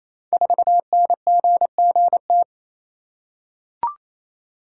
Morse code, heard as a keyed tone, about 29 wpm, 700 Hz. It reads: 4NGGT
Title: Code Groups - 5 Letter:Numbers - 28wpm